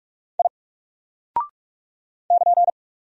Morse code, heard as a keyed tone, about 43 wpm, 700 Hz. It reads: I <KN>
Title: Single Character (Letter:Number:Punctuation:Prosign) Alt 45wpm